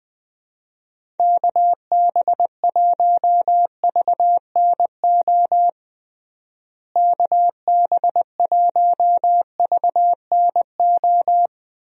Morse code, heard as a keyed tone, about 20 wpm, 700 Hz. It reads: KB1VNO KB1VNO
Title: USA Call Signs - 6 Characters - Repeated 3x - 20wpm